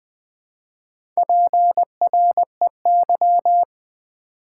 Morse code, heard as a keyed tone, about 20 wpm, 690 Hz. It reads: PREY
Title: Build A Word - 1 Letter At A Time - 4-Letter Words - 20wpm